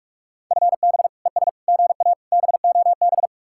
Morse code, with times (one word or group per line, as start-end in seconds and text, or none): 0.5-1.1 FB
1.3-1.5 ES
1.7-2.1 GA
2.3-3.3 BOB